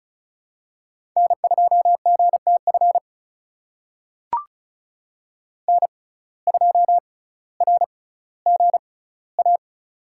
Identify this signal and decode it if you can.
Morse code, heard as a keyed tone, about 36 wpm, spaced as for 18 wpm, 700 Hz. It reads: N2GTF N2RGA
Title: USA Call Signs - 5 Characters - 6x Character Spacing - 35wpm